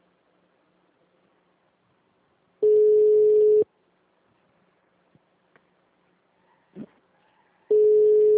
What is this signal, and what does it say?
Ringback tone (Europe, 425 Hz).